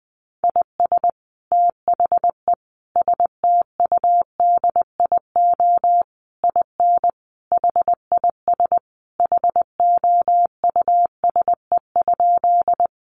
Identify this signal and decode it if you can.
Morse code, heard as a keyed tone, about 20 wpm, 700 Hz. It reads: IS THE STUDIO IN HIS HOUSE?
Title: Sentences from Top 1200 Words 20wpm